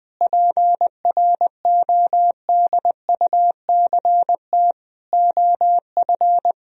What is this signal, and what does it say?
Morse code, heard as a keyed tone, about 20 wpm, 700 Hz. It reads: PRODUCT OF
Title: Sentences from Top 1000 Words - Review Part 2 - 20wpm